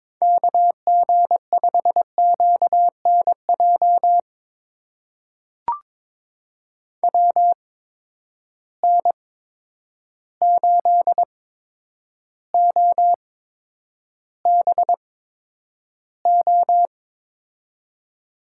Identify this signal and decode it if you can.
Morse code, heard as a keyed tone, about 22 wpm, spaced as for 9 wpm, 700 Hz. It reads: KG5QNJ WN8OBO
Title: USA Call Signs - 8x Character Spacing - 22wpm